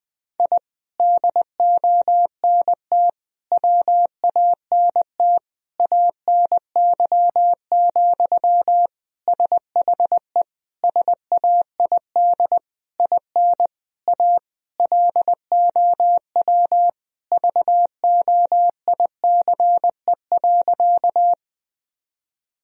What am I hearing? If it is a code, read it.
Morse code, heard as a keyed tone, about 20 wpm, 700 Hz. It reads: I DONT WANT ANY, SHE SAID IN A LOW VOICE.
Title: Sentences from Top 1000 Words - Review Part 1 - 20wpm